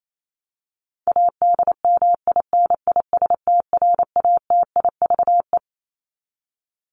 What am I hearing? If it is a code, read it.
Morse code, heard as a keyed tone, about 28 wpm, 700 Hz. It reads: ADMINISTRATIVE
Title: Top 1501-1750 Words 28wpm